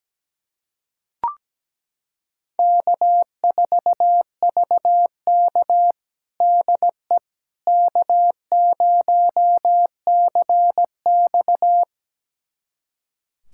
Morse code, heard as a keyed tone, about 17 wpm, 700 Hz. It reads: K4VK DE K0CX
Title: USA Callsign DE Callsign - 4 Characters - 17wpm